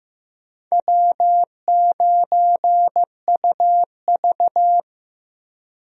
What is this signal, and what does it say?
Morse code, heard as a keyed tone, about 15 wpm, 700 Hz. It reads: W9UV
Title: USA Call Signs - 4 Characters - Repeated 2x - 15wpm